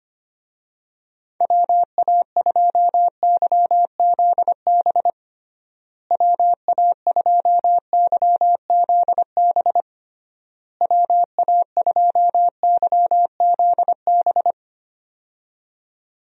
Morse code, heard as a keyed tone, about 25 wpm, 700 Hz. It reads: WA2YZB WA2YZB WA2YZB
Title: USA Call Signs - 6 Characters - Repeated 3x - 25wpm